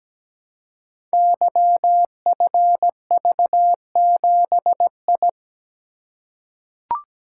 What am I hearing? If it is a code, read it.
Morse code, heard as a keyed tone, about 17 wpm, 700 Hz. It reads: YFV7I
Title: Code Groups - 5 Letter:Numbers - 17wpm